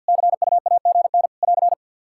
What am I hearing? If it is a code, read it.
Morse code, heard as a keyed tone, about 52 wpm, 700 Hz. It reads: CUAGN <AR>
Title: K1USN SST QSO Practice 50wpm